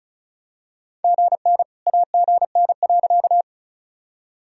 Morse code, heard as a keyed tone, about 34 wpm, 700 Hz. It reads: GN AGN.
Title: Sentences from Top 300 Words and Abbreviations 35wpm